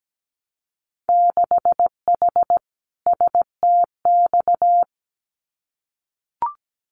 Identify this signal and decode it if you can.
Morse code, heard as a keyed tone, about 17 wpm, 700 Hz. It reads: 6H STX
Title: Winter Field Day Exchanges Encore 17wpm